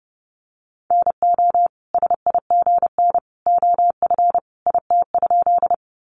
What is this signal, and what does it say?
Morse code, heard as a keyed tone, about 30 wpm, 700 Hz. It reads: NO SIGN OF IT?
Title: Sentences from Top 1000 Words 30wpm